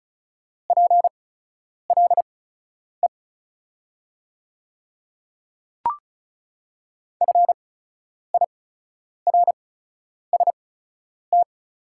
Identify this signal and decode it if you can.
Morse code, heard as a keyed tone, about 36 wpm, spaced as for 10 wpm, 700 Hz. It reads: PLE FIRST
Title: 8x Character Spacing - Top 100 Words - 35wpm